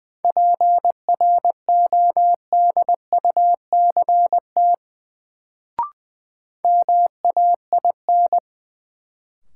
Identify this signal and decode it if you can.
Morse code, heard as a keyed tone, about 20 wpm, 700 Hz. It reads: PRODUCT MAIN
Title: Sentences from Top 1000 Words - Review Part 2 - New Word Drill - 20wpm